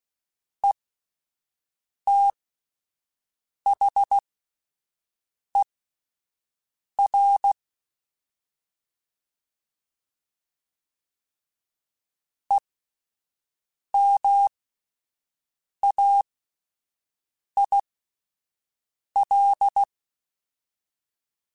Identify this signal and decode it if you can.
Morse code, heard as a keyed tone, about 16 wpm, spaced as for 5 wpm, 800 Hz. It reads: ETHER EMAIL
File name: Alphabetical / Alphabetical mp3s / Es @5